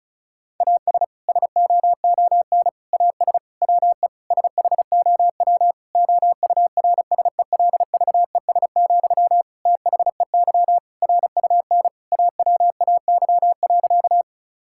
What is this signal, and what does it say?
Morse code, heard as a keyed tone, about 34 wpm, 700 Hz. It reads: AS SOON AS WE SHOW OURSELVES, THEY RUN AWAY.
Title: Sentences from Top 1600 Words 35wpm